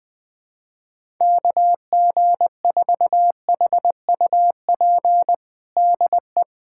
Morse code, heard as a keyed tone, about 20 wpm, 700 Hz. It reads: KG4HUP DE
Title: USA Callsign DE Callsign - 6 Characters - 20wpm